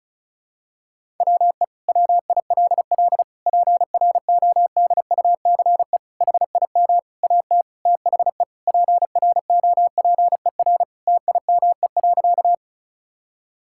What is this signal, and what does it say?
Morse code, heard as a keyed tone, about 34 wpm, 700 Hz. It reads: WE WILL PRODUCE HIM AT THE PROPER TIME.
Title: Sentences from Top 1000 Words - Review Part 5 - 35wpm